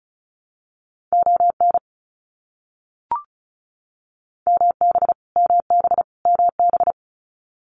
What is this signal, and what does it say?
Morse code, heard as a keyed tone, about 34 wpm, 700 Hz. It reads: ON MB MB MB
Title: Canadian Provinces and Territories - Repeated 3x - 35wpm